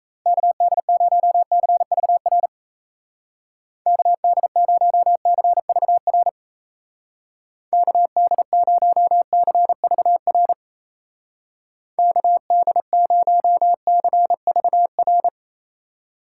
Morse code, heard as a keyed tone, about 34 wpm, 700 Hz. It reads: KD0CVR KD0CVR KD0CVR KD0CVR
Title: Speed Racing - USA 2x3 Call Signs 28wpm